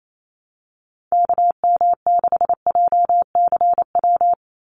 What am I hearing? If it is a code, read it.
Morse code, heard as a keyed tone, about 28 wpm, 700 Hz. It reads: KM6JCW
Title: USA Call Signs 28wpm